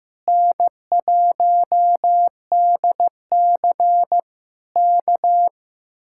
Morse code, heard as a keyed tone, about 15 wpm, 700 Hz. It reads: N1DC K
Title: Speed-Racing Common QSO Phrases Encore - 15wpm